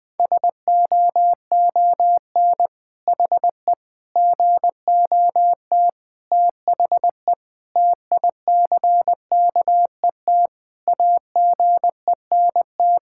Morse code, heard as a keyed tone, about 20 wpm, 700 Hz. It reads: SOON HE GOT THE TICKET AGENT
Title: Sentences from Top 1600 Words 20wpm